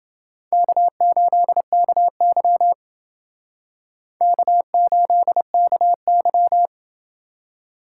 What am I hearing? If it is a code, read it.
Morse code, heard as a keyed tone, about 28 wpm, 700 Hz. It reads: K8KY K8KY